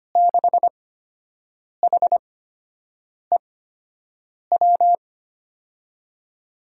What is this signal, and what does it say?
Morse code, heard as a keyed tone, about 25 wpm, spaced as for 7 wpm, 700 Hz. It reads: 6HEW
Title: USA Call Signs - 8x Character Spacing - 25wpm